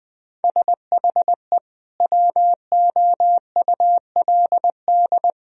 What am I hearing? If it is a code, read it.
Morse code, heard as a keyed tone, about 20 wpm, 700 Hz. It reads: SHE WOULD
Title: Sentences from Top 1000 Words - Review Part 1 - 20wpm